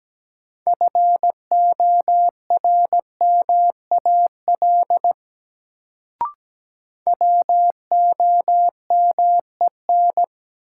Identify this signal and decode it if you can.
Morse code, heard as a keyed tone, about 17 wpm, 700 Hz. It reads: FORMAL WOMEN
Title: Sentences from Top 1300 Words - New Word Drill - 17wpm